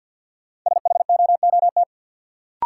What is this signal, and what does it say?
Morse code, heard as a keyed tone, about 53 wpm, 700 Hz. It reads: SHOOT
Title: Sentences from Top 1200 Words - New Word Drill Alt - 50wpm